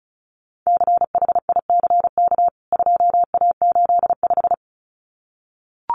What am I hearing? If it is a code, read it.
Morse code, heard as a keyed tone, about 34 wpm, 700 Hz. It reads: CHICK 2A85
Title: CWT Exchange 35wpm